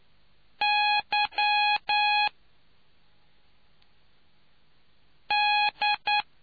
Morse code, heard as a keyed tone, about 9 wpm, spaced as for 3 wpm, 830 Hz. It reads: YD